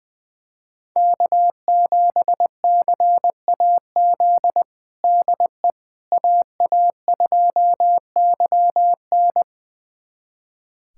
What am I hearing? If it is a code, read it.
Morse code, heard as a keyed tone, about 20 wpm, 700 Hz. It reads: K7CAZ DE AA2YN
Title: USA Callsign DE Callsign - 5 Characters - 20wpm